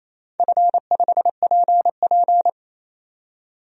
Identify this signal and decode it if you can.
Morse code, heard as a keyed tone, about 28 wpm, 700 Hz. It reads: F5PP
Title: DXCC-100 Call Signs - 4 Characters - 28wpm